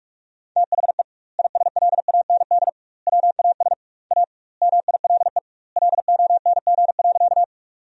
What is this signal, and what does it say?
Morse code, heard as a keyed tone, about 42 wpm, 700 Hz. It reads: THE ISLAND WAS A MILE LONG.
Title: 2x Word-Spacing - Sentences from Top 500 Words - 45wpm